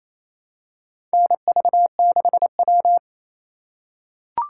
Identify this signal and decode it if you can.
Morse code, heard as a keyed tone, about 28 wpm, 700 Hz. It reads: NV6W
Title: USA Call Signs - 4 Characters - 2x Character Spacing - 28wpm